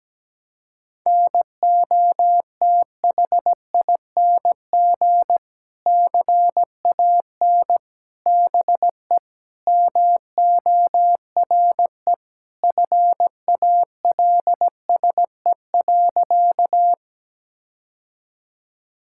Morse code, heard as a keyed tone, about 17 wpm, 700 Hz. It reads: NOTHING CAN BE MORE FALSE.
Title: Sentences from Top 1600 Words 17wpm